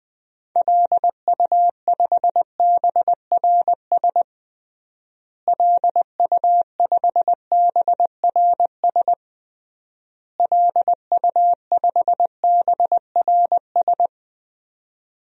Morse code, heard as a keyed tone, about 20 wpm, 700 Hz. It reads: LU5BRS LU5BRS LU5BRS
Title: DXCC-100 Call Signs - 6 Characters - Repeated 3x - 20wpm